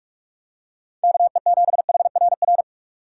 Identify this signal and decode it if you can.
Morse code, heard as a keyed tone, about 44 wpm, 700 Hz. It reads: KE7HRR